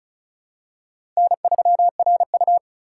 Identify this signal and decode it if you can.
Morse code, heard as a keyed tone, about 34 wpm, 700 Hz. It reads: N3RU
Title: USA Call Signs - 4 Characters - 2x Character Spacing - 35wpm